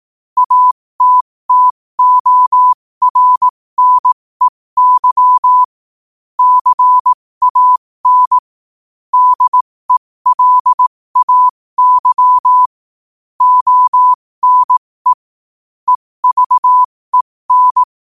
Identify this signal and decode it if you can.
Morse code, heard as a keyed tone, about 18 wpm, spaced as for 15 wpm, 1000 Hz. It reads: ATTORNEY CAN DELAY ONE EVEN
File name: qotdfiles / 15wpm